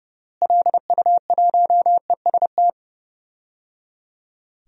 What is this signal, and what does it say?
Morse code, heard as a keyed tone, about 30 wpm, 700 Hz. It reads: LU1EST
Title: DXCC-100 Call Signs - 6 Characters - Repeated 3x - 30wpm